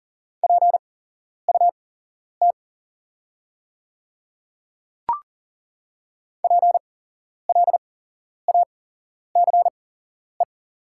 Morse code, heard as a keyed tone, about 40 wpm, spaced as for 11 wpm, 700 Hz. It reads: PUT PLACE
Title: 8x Character Spacing - Top 200 Words - 40wpm